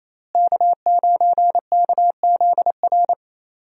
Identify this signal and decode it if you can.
Morse code, heard as a keyed tone, about 28 wpm, 700 Hz. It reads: K9KZR
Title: USA Call Signs - 5 Characters - Repeated 2x - 28wpm